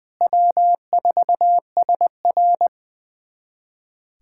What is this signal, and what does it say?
Morse code, heard as a keyed tone, about 21 wpm, 700 Hz. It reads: W4SR
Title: USA Callsign DE Callsign - 4 Characters - 20wpm